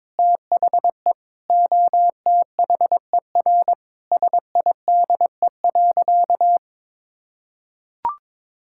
Morse code, heard as a keyed tone, about 22 wpm, 700 Hz. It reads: THE OTHER SIDE.
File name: practice / 22wpm / Sentences with 4-Word Phrases - 22wpm